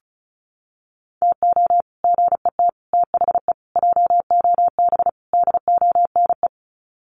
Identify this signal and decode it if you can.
Morse code, heard as a keyed tone, about 34 wpm, 700 Hz. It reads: TO GET THE JOB DONE
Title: Sets of 5 Words Encore2 35wpm